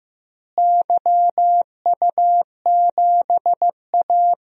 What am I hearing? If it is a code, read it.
Morse code, heard as a keyed tone, about 15 wpm, 700 Hz. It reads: YU7A
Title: DXCC-100 Call Signs - 4 Characters - Repeated 3x - 15wpm